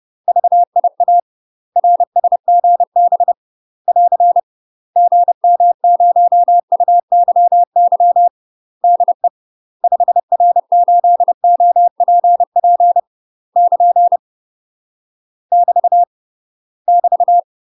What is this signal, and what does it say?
Morse code, heard as a keyed tone, about 30 wpm, 700 Hz. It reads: VIA RSGB <AR> GM0UYY DE 5R8OPP <KN> <BT> <BT>
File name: UK MASTER QSOs TRAINING / 30WPM / UK MASTER QSOs 0034